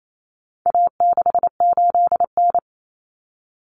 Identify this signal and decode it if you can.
Morse code, heard as a keyed tone, about 28 wpm, 700 Hz. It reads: A68N
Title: Common Cut Numbers - 4-Digit 28wpm